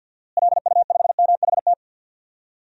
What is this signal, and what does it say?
Morse code, heard as a keyed tone, about 50 wpm, 700 Hz. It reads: LU5MHT
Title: DXCC-100 Call Signs - 6 Characters - Repeated 2x - 50wpm